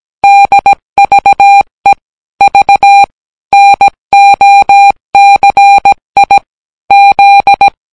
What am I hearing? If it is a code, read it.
Morse code, heard as a keyed tone, about 17 wpm, 800 Hz. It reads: DVE V NOCI Z